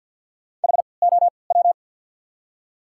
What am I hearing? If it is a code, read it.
Morse code, heard as a keyed tone, about 48 wpm, spaced as for 28 wpm, 700 Hz. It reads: HOW
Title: 3x Character Spacing - Top 100 Words - 50wpm